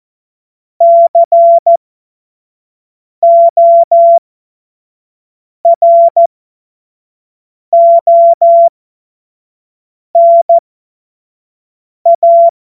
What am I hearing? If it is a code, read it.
Morse code, heard as a keyed tone, about 14 wpm, spaced as for 5 wpm, 680 Hz. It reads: CORONA
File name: Morsekurs / Witze / Witz 17. August 2021, 25 F70